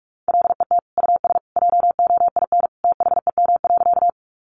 Morse code, heard as a keyed tone, about 47 wpm, 700 Hz. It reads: LET US JOIN THEM.
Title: Sentences from Top 1400 Words 45wpm